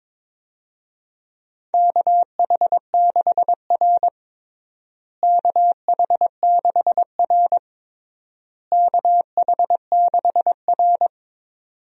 Morse code, heard as a keyed tone, about 22 wpm, 700 Hz. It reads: KH6R KH6R KH6R
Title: USA Call Signs - 4 Characters - Repeated 3x - 22wpm